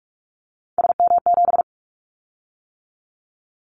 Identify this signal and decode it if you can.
Morse code, heard as a keyed tone, about 44 wpm, 700 Hz. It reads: SM7
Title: Callsign Prefixes for Top 25 DXCC - 45wpm